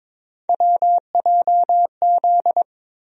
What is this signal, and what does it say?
Morse code, heard as a keyed tone, about 22 wpm, 700 Hz. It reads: WJZ
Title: USA Call Sign Postfixes - 3 Letters 22wpm